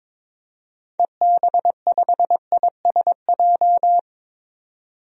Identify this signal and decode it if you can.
Morse code, heard as a keyed tone, about 22 wpm, 700 Hz. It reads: EB5ISJ
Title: DXCC-100 Call Signs - 6 Characters - 22wpm